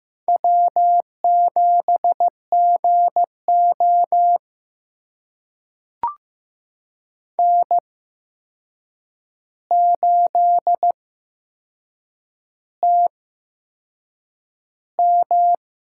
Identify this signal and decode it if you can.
Morse code, heard as a keyed tone, about 15 wpm, spaced as for 6 wpm, 700 Hz. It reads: W7GO N8TM
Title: USA Call Signs - 4 Characters - 8x Character Spacing - 15wpm